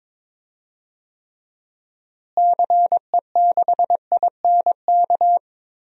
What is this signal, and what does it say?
Morse code, heard as a keyed tone, about 22 wpm, 700 Hz. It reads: CE6INK
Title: DXCC-100 Call Signs - 6 Characters - Repeated 2x - 22wpm